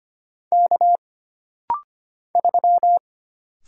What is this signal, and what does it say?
Morse code, heard as a keyed tone, about 25 wpm, 700 Hz. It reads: K 3
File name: Single Character (Letter:Number:Punctuation:Prosign) - Rapid-Fire - 25wpm